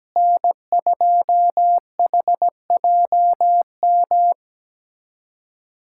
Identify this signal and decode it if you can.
Morse code, heard as a keyed tone, about 17 wpm, 700 Hz. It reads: N2HJM
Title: USA Callsign DE Callsign - 5 Characters - 17wpm